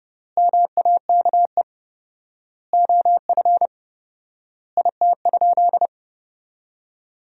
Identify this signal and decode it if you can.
Morse code, heard as a keyed tone, about 30 wpm, 700 Hz. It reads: MAKE OF IT?
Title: 4x Word-Spacing - Sentences from Top 100 Words - 30wpm